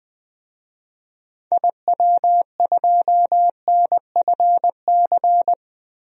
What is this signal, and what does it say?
Morse code, heard as a keyed tone, about 20 wpm, 700 Hz. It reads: IW2NFC
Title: DXCC-100 Call Signs - 6 Characters - 20wpm